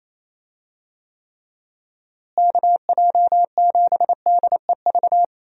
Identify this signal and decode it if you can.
Morse code, heard as a keyed tone, about 27 wpm, 700 Hz. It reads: KJ7DEV
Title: USA Call Signs - 6 Characters - Repeated 2x - 28wpm